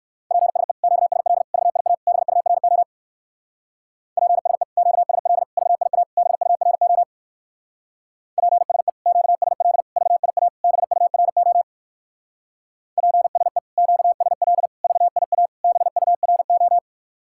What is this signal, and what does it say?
Morse code, heard as a keyed tone, about 58 wpm, 700 Hz. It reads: PSE QSL VIA BURO PSE QSL VIA BURO PSE QSL VIA BURO PSE QSL VIA BURO